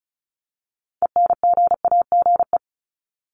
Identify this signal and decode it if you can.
Morse code, heard as a keyed tone, about 34 wpm, 700 Hz. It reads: ENGAGE
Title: Top 1251-1500 Words 35wpm